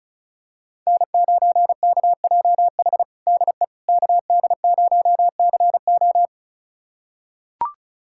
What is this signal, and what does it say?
Morse code, heard as a keyed tone, about 36 wpm, 700 Hz. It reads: N9KJH DE KD0CO
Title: USA Callsign DE Callsign - 5 Characters - 35wpm